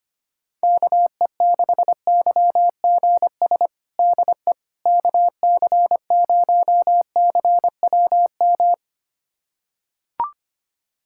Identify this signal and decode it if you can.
Morse code, heard as a keyed tone, about 25 wpm, 700 Hz. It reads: KE6YGS DE KC0CWM
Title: USA Callsign DE Callsign 25wpm